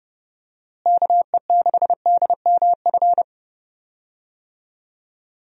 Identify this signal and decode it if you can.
Morse code, heard as a keyed tone, about 30 wpm, 700 Hz. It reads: KE6DMF
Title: USA Call Signs - 6 Characters - Repeated 3x - 30wpm